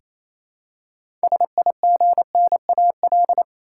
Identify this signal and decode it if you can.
Morse code, heard as a keyed tone, about 28 wpm, 700 Hz. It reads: SIGNAL
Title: Six Letter Words - 28wpm